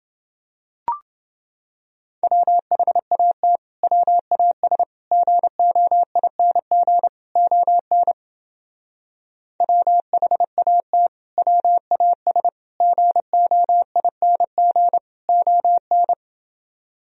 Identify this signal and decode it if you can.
Morse code, heard as a keyed tone, about 28 wpm, 700 Hz. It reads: WHAT WAS GOING ON WHAT WAS GOING ON